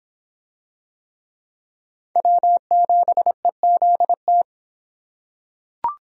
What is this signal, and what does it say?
Morse code, heard as a keyed tone, about 26 wpm, 700 Hz. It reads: W7EZT